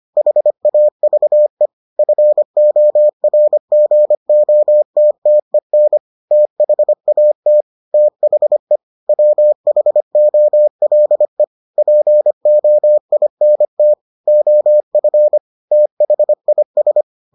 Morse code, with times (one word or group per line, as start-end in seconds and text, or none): 0.2-1.7 HAVE
2.0-6.0 FORGOTTEN
6.3-7.6 THAT
7.9-8.8 THE
9.1-11.4 WHOLE
11.8-13.9 POINT
14.3-15.4 OF
15.7-17.0 THIS